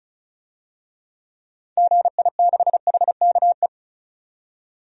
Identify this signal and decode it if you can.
Morse code, heard as a keyed tone, about 36 wpm, 700 Hz. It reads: GI6HKE
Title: DXCC-100 Call Signs - 6 Characters - 35wpm